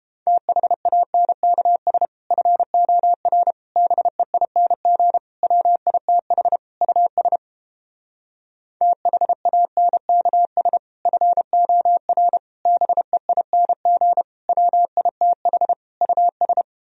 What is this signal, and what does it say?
Morse code, heard as a keyed tone, about 29 wpm, 700 Hz. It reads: THANKS FOR BEING WITH US THANKS FOR BEING WITH US